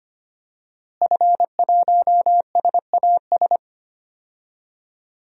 Morse code, heard as a keyed tone, about 25 wpm, 700 Hz. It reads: F1SAS
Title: DXCC-100 Call Signs - 5 Characters - Repeated 3x - 25wpm